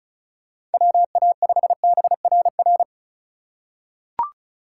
Morse code, heard as a keyed tone, about 36 wpm, 700 Hz. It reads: WA5BRR
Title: USA Call Signs - 6 Characters - 4x Character Spacing - 35wpm